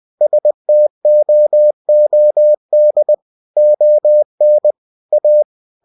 Morse code, heard as a keyed tone, about 20 wpm, 600 Hz. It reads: STOOD ON A
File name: cw-books / winnie-the-pooh / winnie-20wpm-page-0039